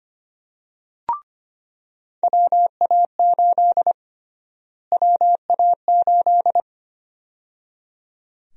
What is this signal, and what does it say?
Morse code, heard as a keyed tone, about 25 wpm, 700 Hz. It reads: WA8 WA8
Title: USA Call Sign Prefixes - Repeated 2x - 25wpm